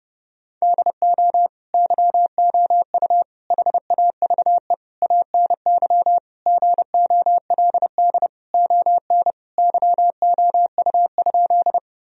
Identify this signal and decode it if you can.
Morse code, heard as a keyed tone, about 30 wpm, 700 Hz. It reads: DO YOU HAVE ANY GOLD ON YOU?
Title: Sentences from Top 1000 Words - Review Part 3 - 30wpm